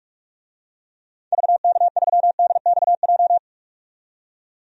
Morse code, heard as a keyed tone, about 43 wpm, 700 Hz. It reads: VK3DXJ
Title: DXCC-100 Call Signs - 6 Characters - 45wpm